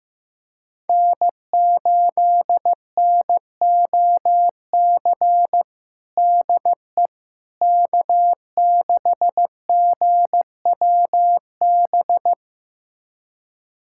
Morse code, heard as a keyed tone, about 15 wpm, 700 Hz. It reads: N8NOC DE K6GWB
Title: USA Callsign DE Callsign - 5 Characters - 15wpm